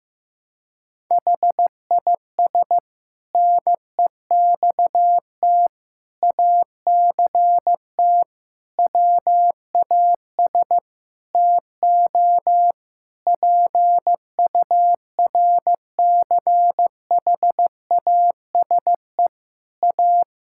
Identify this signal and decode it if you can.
Morse code, heard as a keyed tone, about 15 wpm, 700 Hz. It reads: HIS NEXT ACT WAS TO PURCHASE A
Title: Sentences from Top 1000 Words - Review Part 5 - 15wpm